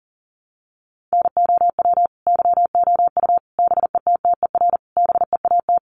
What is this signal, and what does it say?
Morse code, heard as a keyed tone, about 40 wpm, 700 Hz. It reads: NOW YOU BETTER BEAT
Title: Sentences from Top 1300 Words 40wpm